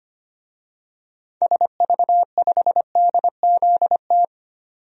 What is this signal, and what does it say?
Morse code, heard as a keyed tone, about 25 wpm, 700 Hz. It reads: SV5DZT
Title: DXCC-100 Call Signs - 6 Characters - 25wpm